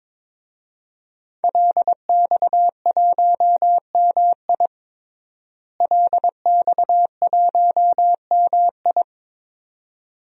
Morse code, heard as a keyed tone, about 22 wpm, 700 Hz. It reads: LX1MI LX1MI
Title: DXCC-100 Call Signs - 5 Characters - Repeated 2x - 22wpm